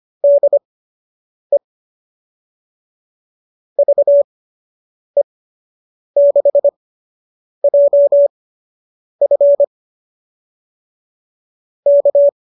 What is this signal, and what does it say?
Morse code, heard as a keyed tone, about 25 wpm, spaced as for 8 wpm, 570 Hz. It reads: DE VE6JF K